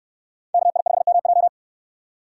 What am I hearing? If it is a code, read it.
Morse code, heard as a keyed tone, about 65 wpm, 700 Hz. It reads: KI5MJ